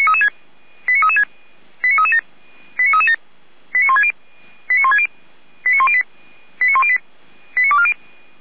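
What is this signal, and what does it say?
Five-tone selective call (ZVEI), heard as quick runs of five tones, tones of about 70 ms. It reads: 89336 89337 89337 89160 89160 89108 89108 89240
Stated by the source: ZVEI five-tone selective call received with an SDR.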